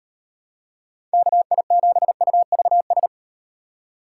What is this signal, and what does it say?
Morse code, heard as a keyed tone, about 36 wpm, 700 Hz. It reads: KI7UVS